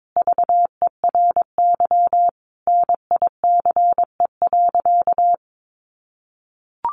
Morse code, heard as a keyed tone, about 22 wpm, 700 Hz. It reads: VERY NICE.
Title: Sentences from Top 1100 Words 22wpm